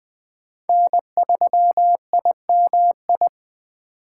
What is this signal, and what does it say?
Morse code, heard as a keyed tone, about 20 wpm, 700 Hz. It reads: N3IMI
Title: USA Call Signs - 4x Character Spacing - 20wpm